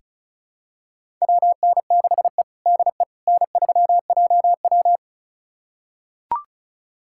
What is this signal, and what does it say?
Morse code, heard as a keyed tone, about 36 wpm, 700 Hz. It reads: WN6E DE N3JW
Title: USA Callsign DE Callsign - 4 Characters - 35wpm